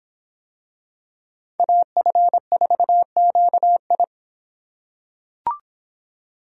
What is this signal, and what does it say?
Morse code, heard as a keyed tone, about 25 wpm, 700 Hz. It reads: AF4QI